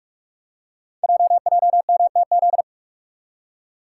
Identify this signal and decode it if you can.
Morse code, heard as a keyed tone, about 44 wpm, 700 Hz. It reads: JJMTZ
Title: Code Groups - 5 Letters - 45wpm